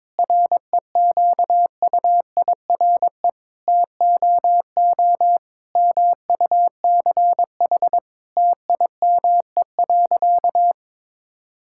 Morse code, heard as a keyed tone, about 22 wpm, 700 Hz. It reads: REQUIRE TOO MUCH TIME.
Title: Sentences from Top 1000 Words - Review Part 6 - 22wpm